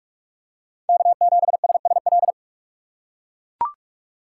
Morse code, heard as a keyed tone, about 43 wpm, 700 Hz. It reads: K7SSL
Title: USA Call Signs - 5 Characters - 2x Character Spacing - 45wpm